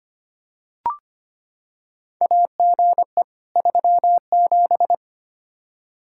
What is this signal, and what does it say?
Morse code, heard as a keyed tone, about 25 wpm, 700 Hz. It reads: AGE 37
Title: K1USN SST QSO Practice 25wpm